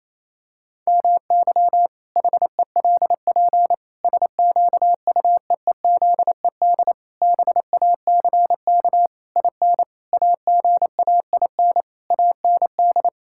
Morse code, heard as a keyed tone, about 28 wpm, 700 Hz. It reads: MY HELP SQUEEZED BACK IN AGAIN AND
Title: Pangram Challenge v2 - 28wpm